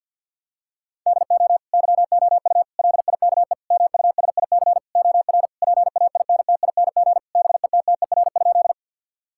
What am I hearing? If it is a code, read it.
Morse code, heard as a keyed tone, about 50 wpm, 700 Hz. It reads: DO YOU LIKE MUSIC OR PAINTING BETTER?
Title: Sentences from Top 1000 Words - Review Part 6 - 50wpm